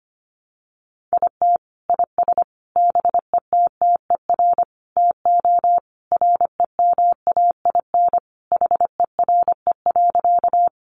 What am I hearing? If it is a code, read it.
Morse code, heard as a keyed tone, about 25 wpm, 700 Hz. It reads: IT IS BETTER TO REMAIN HERE.
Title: Sentences from Top 1200 Words 25wpm